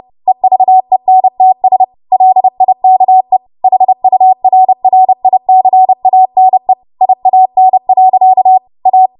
Morse code, heard as a keyed tone, about 30 wpm, 750 Hz. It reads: EVENTS LIKE HURRICANE IAN. A
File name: Morse / 30WPM